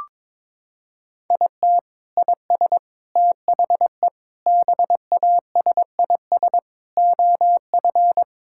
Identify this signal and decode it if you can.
Morse code, heard as a keyed tone, about 22 wpm, 700 Hz. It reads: IT IS THE BASIS OF
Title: Sentences from Top 1000 Words - Review Part 5 - 22wpm